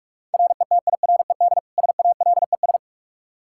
Morse code, heard as a keyed tone, about 46 wpm, 700 Hz. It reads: RETIRED SALES
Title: Practice QSO - Exchange 3 - More Info - 45wpm